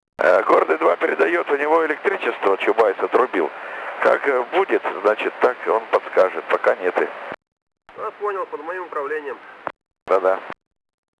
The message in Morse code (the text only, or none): none